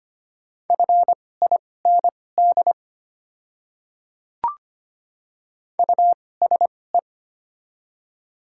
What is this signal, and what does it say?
Morse code, heard as a keyed tone, about 25 wpm, spaced as for 18 wpm, 700 Hz. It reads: FIND USE
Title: 2x Character Spacing - Top 200 Words - 25wpm